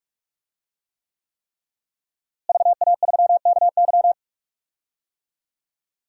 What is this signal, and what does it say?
Morse code, heard as a keyed tone, about 46 wpm, 700 Hz. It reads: VA3KY
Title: DXCC-100 Call Signs - 5 Characters - Repeated 3x - 45wpm